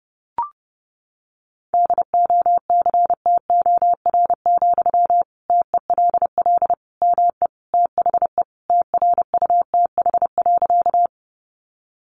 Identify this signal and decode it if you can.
Morse code, heard as a keyed tone, about 28 wpm, 700 Hz. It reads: DOCTOR, TELL ME THE TRUTH.
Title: Sentences from Top 1500 Words 30wpm